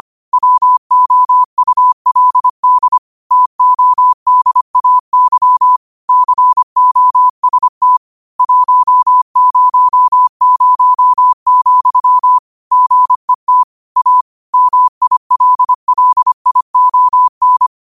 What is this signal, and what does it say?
Morse code, heard as a keyed tone, about 25 wpm, 1000 Hz. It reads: WOULD TODAY COST 100, GET A MILLION
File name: qotdfiles / 25wpm